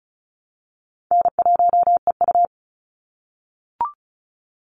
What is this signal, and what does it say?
Morse code, heard as a keyed tone, about 36 wpm, 700 Hz. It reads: N1EU
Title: NA SOTA Activator Callsigns 35wpm